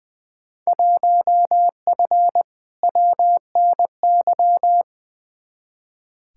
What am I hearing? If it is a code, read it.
Morse code, heard as a keyed tone, about 21 wpm, 700 Hz. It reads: 1F WNY
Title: Field Day Exchanges - 20wpm